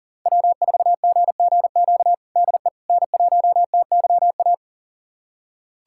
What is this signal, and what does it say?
Morse code, heard as a keyed tone, about 40 wpm, 700 Hz. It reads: W4GGQ DE N1TYA
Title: USA Callsign DE Callsign - 5 Characters - 40wpm